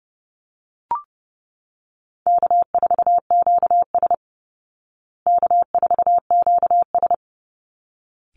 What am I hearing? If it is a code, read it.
Morse code, heard as a keyed tone, about 30 wpm, 700 Hz. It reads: K4QS K4QS
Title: Easy Call Signs Repeated 2x - 30wpm